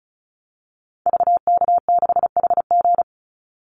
Morse code, heard as a keyed tone, about 34 wpm, 700 Hz. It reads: VK6HG
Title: Call Signs - 35wpm